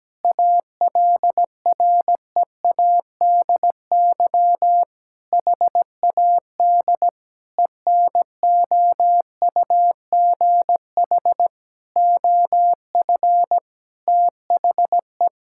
Morse code, heard as a keyed tone, about 17 wpm, 700 Hz. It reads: ALREADY HAD ENOUGH OF THE
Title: Sentences from Top 1600 Words 17wpm